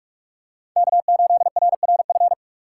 Morse code, heard as a keyed tone, about 46 wpm, 700 Hz. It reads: K8RRF
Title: USA Call Signs - 5 Characters - 2x Character Spacing - 45wpm